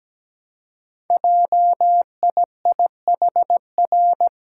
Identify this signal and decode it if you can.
Morse code, heard as a keyed tone, about 17 wpm, 700 Hz. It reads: JIIHR
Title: Code Groups - 5 Letters - 17wpm